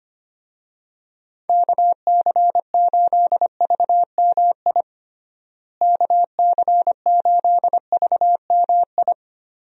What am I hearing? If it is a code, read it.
Morse code, heard as a keyed tone, about 25 wpm, 700 Hz. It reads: KC8VMI KC8VMI
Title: USA Call Signs - 6 Characters - Repeated 3x - 25wpm